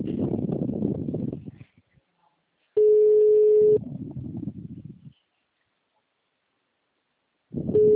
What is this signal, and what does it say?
Ringback tone (Europe, 425 Hz).
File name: out- -2007- - - .83